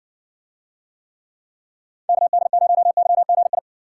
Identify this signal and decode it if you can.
Morse code, heard as a keyed tone, about 60 wpm, 700 Hz. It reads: KD0YGI